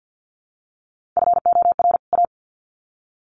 Morse code, heard as a keyed tone, about 50 wpm, 700 Hz. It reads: FOR A
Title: Sets of 2 Words 50wpm